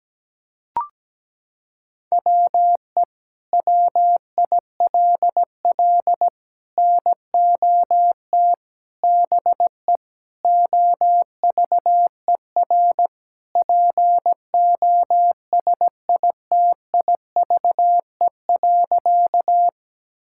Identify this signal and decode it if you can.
Morse code, heard as a keyed tone, about 17 wpm, 700 Hz. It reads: WE WILL NOT BE OVER POSITIVE.
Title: Sentences from Top 1000 Words - Review Part 5 - 17wpm